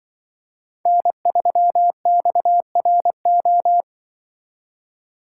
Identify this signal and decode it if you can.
Morse code, heard as a keyed tone, about 24 wpm, 700 Hz. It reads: N3XRO
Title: Speed Racing - USA 1x3 Call Signs 20wpm